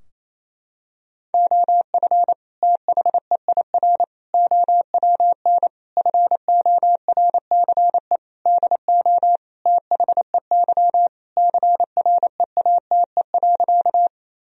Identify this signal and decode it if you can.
Morse code, heard as a keyed tone, about 28 wpm, 700 Hz. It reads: OF THEIR OWN FORCE DO THEY CREATE.
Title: Sentences from Top 1000 Words - Review Part 3 - 28wpm